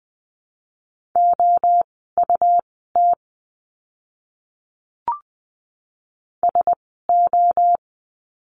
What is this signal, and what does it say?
Morse code, heard as a keyed tone, about 20 wpm, spaced as for 14 wpm, 700 Hz. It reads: OUT SO
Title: 230 Lesson 37f - Letter x - Words 20wpm